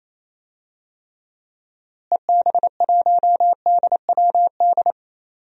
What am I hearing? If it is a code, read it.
Morse code, heard as a keyed tone, about 28 wpm, 700 Hz. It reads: EB1DWD
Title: DXCC-100 Call Signs - 6 Characters - Repeated 3x - 28wpm